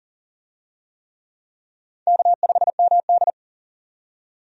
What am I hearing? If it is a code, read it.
Morse code, heard as a keyed tone, about 40 wpm, 700 Hz. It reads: K5MD
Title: USA Call Signs - 4 Characters - 40wpm